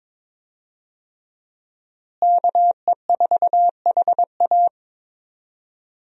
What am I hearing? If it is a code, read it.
Morse code, heard as a keyed tone, about 22 wpm, 700 Hz. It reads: KE4HA
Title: USA Call Signs - 5 Characters - Repeated 3x - 22wpm